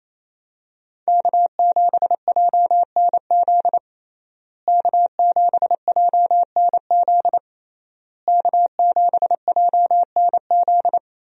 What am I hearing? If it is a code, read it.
Morse code, heard as a keyed tone, about 28 wpm, 700 Hz. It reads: K7JNZ K7JNZ K7JNZ
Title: USA Call Signs - 5 Characters - Repeated 3x - 28wpm